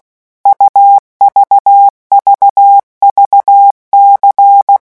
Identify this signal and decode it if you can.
Morse code, heard as a keyed tone, about 16 wpm, 800 Hz. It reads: UVVVC
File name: morse mp3 / revision2-vitesse 15 mots